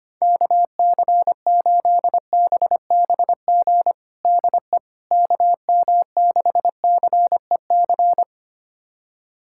Morse code, heard as a keyed tone, about 25 wpm, 700 Hz. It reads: KC8BBG DE KM6CEC
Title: USA Callsign DE Callsign 25wpm